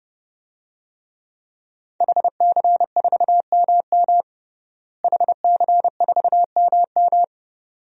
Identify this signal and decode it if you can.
Morse code, heard as a keyed tone, about 32 wpm, 700 Hz. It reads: HC4MM HC4MM
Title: DXCC-100 Call Signs - 5 Characters - Repeated 2x - 30wpm